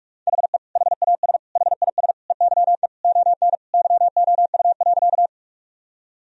Morse code, heard as a keyed tone, about 46 wpm, 700 Hz. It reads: HE HAS HIS EYE ON YOU.
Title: Sentences from Top 1000 Words - Review Part 1 - 45wpm